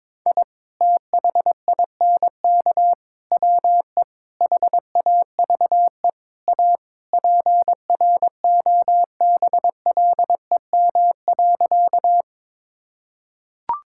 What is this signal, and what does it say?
Morse code, heard as a keyed tone, about 22 wpm, 700 Hz. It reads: I THINK WE HAVE A PROBLEM.
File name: practice / 22wpm / Sentences with 4-Word Phrases - 22wpm